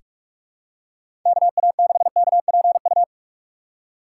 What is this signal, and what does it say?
Morse code, heard as a keyed tone, about 46 wpm, 700 Hz. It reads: KA6KPU
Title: USA Call Signs - 6 Characters - 4x Character Spacing - 45wpm